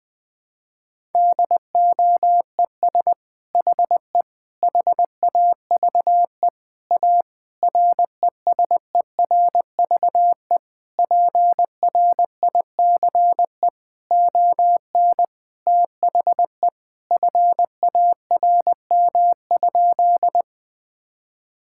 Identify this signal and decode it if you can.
Morse code, heard as a keyed tone, about 20 wpm, 700 Hz. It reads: DOES HE HAVE A RESERVE PRICE ON THE FARM?
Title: Sentences from Top 1000 Words - Review Part 6 - 20wpm